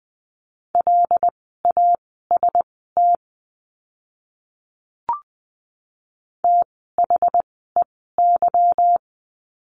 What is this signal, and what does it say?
Morse code, heard as a keyed tone, about 20 wpm, spaced as for 15 wpm, 700 Hz. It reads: LAST THEY
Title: 244 Lesson 39f - Letter z - Words 20wpm